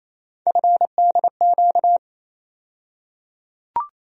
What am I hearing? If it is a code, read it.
Morse code, heard as a keyed tone, about 28 wpm, 700 Hz. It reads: FDQ
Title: USA Call Sign Postfixes - 3 Letters 28wpm